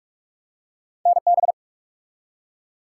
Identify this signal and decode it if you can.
Morse code, heard as a keyed tone, about 44 wpm, 700 Hz. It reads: NB
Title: US States and Canadian Provinces 45wpm